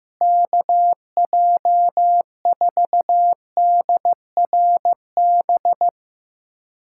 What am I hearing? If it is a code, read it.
Morse code, heard as a keyed tone, about 15 wpm, 700 Hz. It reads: KJ4DRB
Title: USA Callsign DE Callsign 15wpm